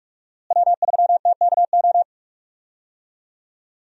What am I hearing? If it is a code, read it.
Morse code, heard as a keyed tone, about 42 wpm, 700 Hz. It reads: W3TKO
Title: USA Call Signs - 5 Characters - Repeated 3x - 45wpm